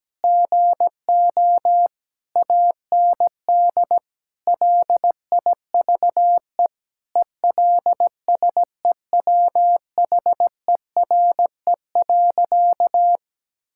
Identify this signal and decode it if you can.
Morse code, heard as a keyed tone, about 17 wpm, 700 Hz. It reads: GO AND LIVE ELSEWHERE.
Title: Sentences from Top 1600 Words 17wpm